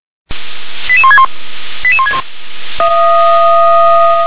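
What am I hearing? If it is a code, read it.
Five-tone selective call (ZVEI), heard as quick runs of five tones, tones of about 70 ms. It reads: 88161 88161
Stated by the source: Five-tone ZVEI selective call received on an SDR.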